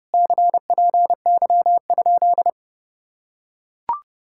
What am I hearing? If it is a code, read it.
Morse code, heard as a keyed tone, about 30 wpm, 700 Hz. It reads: CPY?